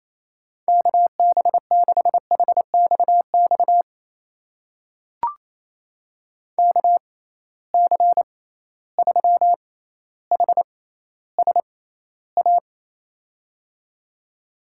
Morse code, heard as a keyed tone, about 28 wpm, spaced as for 14 wpm, 700 Hz. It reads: KB6HXX KC3HSA
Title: USA Call Signs - 6 Characters - 6x Character Spacing - 28wpm